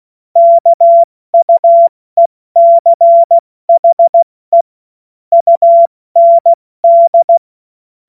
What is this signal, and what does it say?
Morse code, heard as a keyed tone, about 16 wpm, 680 Hz. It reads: KUECHE UND
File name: Morsekurs / Witze / Witz 09 April 2024 70 F80